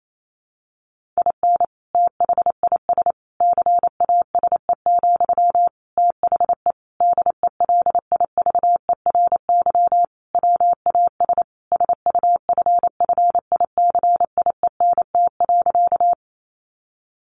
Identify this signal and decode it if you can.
Morse code, heard as a keyed tone, about 28 wpm, 700 Hz. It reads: IN THIS CASE, THE DELIVERY WAS SUFFICIENT.
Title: Sentences from Top 1500 Words 28wpm